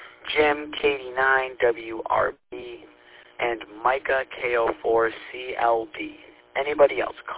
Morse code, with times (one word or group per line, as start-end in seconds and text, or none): none